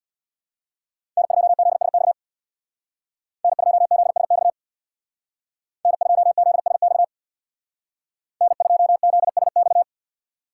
Morse code, heard as a keyed tone, about 61 wpm, 700 Hz. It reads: N2ZSX N2ZSX N2ZSX N2ZSX
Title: Speed Racing - USA 1x3 Call Signs 50wpm